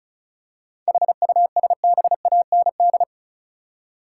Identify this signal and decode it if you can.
Morse code, heard as a keyed tone, about 36 wpm, 700 Hz. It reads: HUSBAND
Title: Seven Letter Words - 35wpm